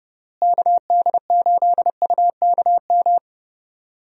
Morse code, heard as a keyed tone, about 30 wpm, 700 Hz. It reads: KD8UKM
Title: USA Call Signs - 6 Characters - Repeated 2x - 30wpm